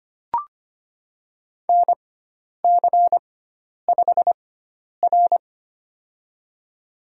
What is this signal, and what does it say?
Morse code, heard as a keyed tone, about 25 wpm, spaced as for 10 wpm, 700 Hz. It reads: NC5R
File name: Lesson 16g - Letter c - Call Signs 25wpm